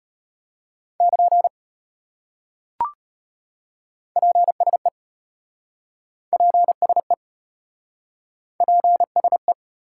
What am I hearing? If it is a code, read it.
Morse code, heard as a keyed tone, about 33 wpm, 700 Hz. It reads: <KN> PSE PSE PSE